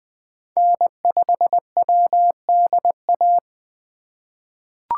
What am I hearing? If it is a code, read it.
Morse code, heard as a keyed tone, about 20 wpm, 700 Hz. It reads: N5WDA
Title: USA Call Signs - 5 Characters - 8x Character Spacing - 20wpm